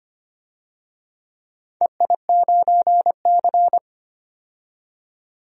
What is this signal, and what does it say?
Morse code, heard as a keyed tone, about 25 wpm, 700 Hz. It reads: EI9C
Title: DXCC-100 Call Signs - 4 Characters - 25wpm